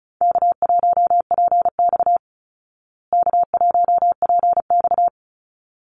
Morse code, heard as a keyed tone, about 34 wpm, 700 Hz. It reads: K1PX K1PX
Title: Call Signs Repeated 3x - 35wpm